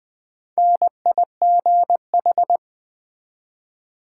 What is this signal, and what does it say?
Morse code, heard as a keyed tone, about 20 wpm, 700 Hz. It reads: NIGH
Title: Build A Word - 1 Letter At A Time - 5-Letter Words - 20wpm